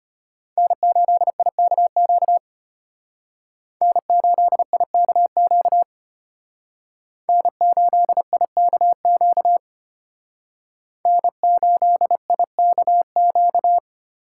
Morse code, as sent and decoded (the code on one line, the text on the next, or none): -. ---.. .. -.- --.- / -. ---.. .. -.- --.- / -. ---.. .. -.- --.- / -. ---.. .. -.- --.-
N8IKQ N8IKQ N8IKQ N8IKQ